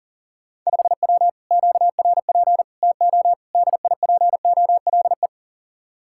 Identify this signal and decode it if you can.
Morse code, heard as a keyed tone, about 49 wpm, 700 Hz. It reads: 5W QRP TO DIPOLE
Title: Speed-Racing Common QSO Phrases - 40wpm